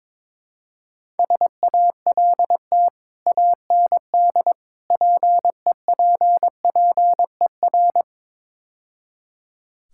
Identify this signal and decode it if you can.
Morse code, heard as a keyed tone, about 22 wpm, 700 Hz. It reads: SALT AND PEPPER
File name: practice / 22wpm / Build A Phrase - 5-Word Phrases - 22wpm